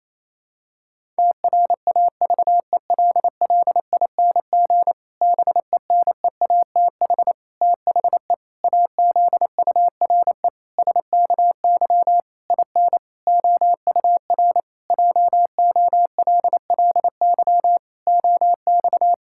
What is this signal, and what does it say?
Morse code, heard as a keyed tone, about 28 wpm, 700 Hz. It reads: TRAVELLING BENEATH THE AZURE SKY IN OUR JOLLY OX
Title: Pangram Challenge v2 - 28wpm